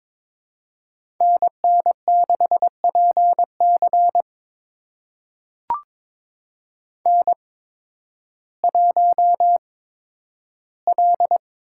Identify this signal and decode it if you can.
Morse code, heard as a keyed tone, about 22 wpm, spaced as for 12 wpm, 700 Hz. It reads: NN6PC N1L
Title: USA Call Signs - 8x Character Spacing - 22wpm